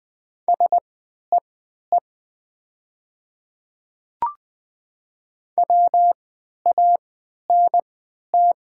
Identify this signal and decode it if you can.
Morse code, heard as a keyed tone, about 20 wpm, spaced as for 11 wpm, 700 Hz. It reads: SEE WANT
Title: 3x Character Spacing - Top 200 Words - 20wpm